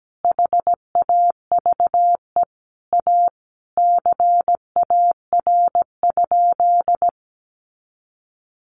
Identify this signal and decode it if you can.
Morse code, heard as a keyed tone, about 17 wpm, 700 Hz. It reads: HAVE A CAR?
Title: Sentences from Top 300 Words 17wpm